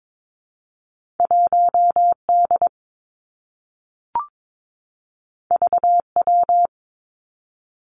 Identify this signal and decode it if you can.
Morse code, heard as a keyed tone, about 22 wpm, 700 Hz. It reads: 1 D V W
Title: Two Challenging Letter-Number Combinations 22wpm